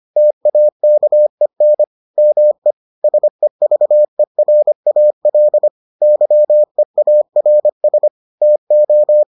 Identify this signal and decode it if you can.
Morse code, heard as a keyed tone, about 25 wpm, 600 Hz. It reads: TAKEN ME SEVERAL YEARS TO